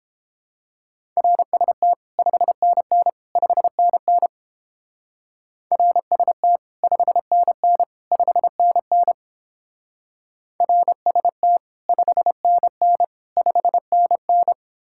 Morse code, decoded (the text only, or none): RST 5NN 5NN RST 5NN 5NN RST 5NN 5NN